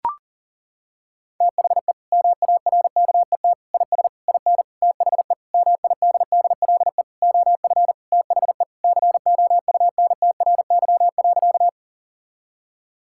Morse code, heard as a keyed tone, about 40 wpm, 700 Hz. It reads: THE MARKET IS IN THE MIDDLE OF THE COUNTRY.
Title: Sentences from Top 1000 Words - Review Part 6 - 40wpm